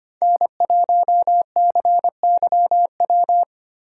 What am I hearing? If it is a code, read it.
Morse code, heard as a keyed tone, about 25 wpm, 700 Hz. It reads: N1CYW
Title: USA Call Signs - 5 Characters - 2x Character Spacing - 25wpm